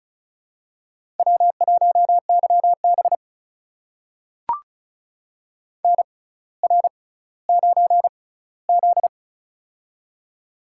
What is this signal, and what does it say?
Morse code, heard as a keyed tone, about 36 wpm, spaced as for 18 wpm, 700 Hz. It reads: W1YB NR9Z
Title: USA Call Signs - 4 Characters - 6x Character Spacing - 35wpm